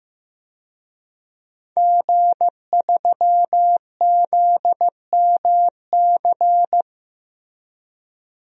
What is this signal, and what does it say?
Morse code, heard as a keyed tone, about 15 wpm, 700 Hz. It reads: G3ZMC
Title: DXCC-100 Call Signs - 5 Characters - 15wpm